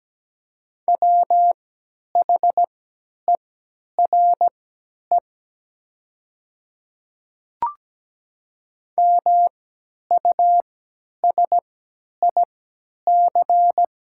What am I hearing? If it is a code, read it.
Morse code, heard as a keyed tone, about 17 wpm, spaced as for 10 wpm, 700 Hz. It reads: WHERE MUSIC
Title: 3x Character Spacing - Top 500 Words - 17wpm